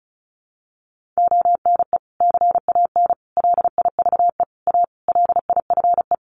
Morse code, heard as a keyed tone, about 34 wpm, 700 Hz. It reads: ONE CAN LIVE A LIFE
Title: Sentences from Top 1400 Words 35wpm